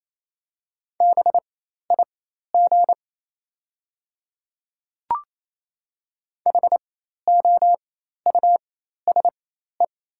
Morse code, heard as a keyed tone, about 28 wpm, spaced as for 13 wpm, 690 Hz. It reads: BIG HOUSE
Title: 4x Character Spacing - Top 300 Words - 28wpm